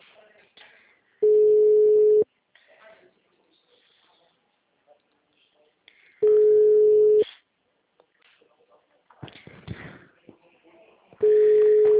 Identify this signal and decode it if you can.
Ringback tone (Europe, 425 Hz), with speech.